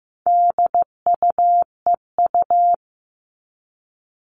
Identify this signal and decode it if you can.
Morse code, heard as a keyed tone, about 15 wpm, 700 Hz. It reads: DUEU
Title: All Cut Numbers - 4-Digit 15wpm